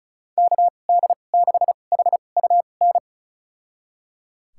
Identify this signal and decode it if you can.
Morse code, heard as a keyed tone, about 34 wpm, spaced as for 25 wpm, 700 Hz. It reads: KD6HUN
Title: USA Call Signs - 6 Characters - 2x Character Spacing - 35wpm